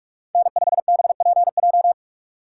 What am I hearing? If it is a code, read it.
Morse code, heard as a keyed tone, about 41 wpm, 690 Hz. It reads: N5BPJ
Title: USA Call Signs - 5 Characters - 6x Character Spacing - 45wpm